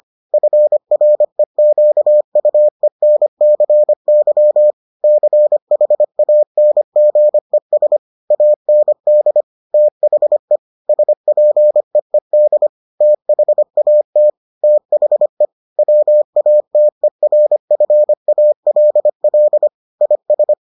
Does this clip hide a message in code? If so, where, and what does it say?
Yes: Morse code from 0.3 to 20.5 s, heard as a keyed tone, about 25 wpm, 600 Hz. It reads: FREQUENCY CHANGES AND THE SPEED THAT THE WATERFALL IS